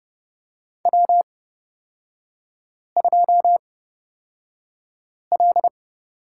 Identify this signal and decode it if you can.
Morse code, heard as a keyed tone, about 29 wpm, spaced as for 5 wpm, 690 Hz. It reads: W2L
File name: Lesson 38e - Letter q - Three Characters 30wpm